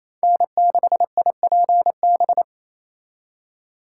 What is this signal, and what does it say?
Morse code, heard as a keyed tone, about 35 wpm, 700 Hz. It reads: N6IPB